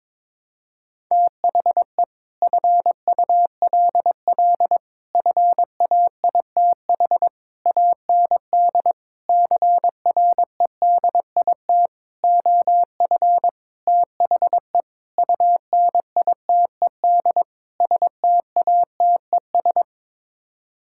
Morse code, heard as a keyed tone, about 22 wpm, 700 Hz. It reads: THE FULL FAITH AND CREDIT OF THE UNITED STATES
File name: Sets of 9 Words - 22wpm